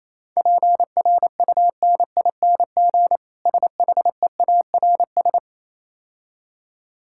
Morse code, heard as a keyed tone, about 27 wpm, 690 Hz. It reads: PRUNING SHEARS
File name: Easy Words - Speed-Racing with Hints - 22wpm